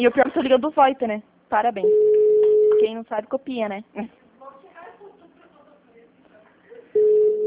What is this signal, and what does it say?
Ringback tone (Europe, 425 Hz), with speech.
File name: out- -2003- - - .86